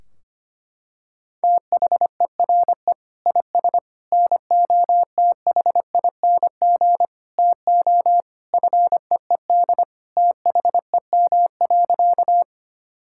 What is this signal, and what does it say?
Morse code, heard as a keyed tone, about 25 wpm, 700 Hz. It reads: THERE IS NOTHING TO FEED THEM.
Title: Sentences from Top 1600 Words 25wpm